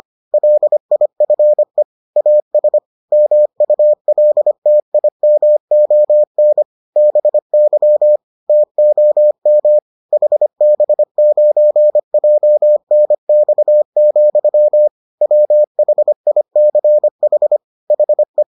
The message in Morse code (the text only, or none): LIFE AS MULTIMON BY TOM HB9JNX, WHICH HE